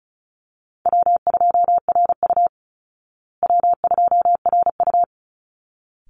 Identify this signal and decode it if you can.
Morse code, heard as a keyed tone, about 34 wpm, 700 Hz. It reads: W2RU W2RU
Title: Easy Call Signs Repeated 2x - 35wpm